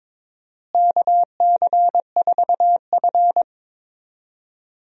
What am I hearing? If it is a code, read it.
Morse code, heard as a keyed tone, about 23 wpm, 700 Hz. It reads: KC4F
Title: USA Call Signs - 4 Characters - Repeated 2x - 22wpm